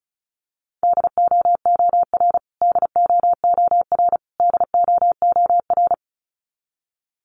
Morse code, heard as a keyed tone, about 34 wpm, 700 Hz. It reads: DOOR DOOR DOOR
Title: Top 500 Words Repeated 35wpm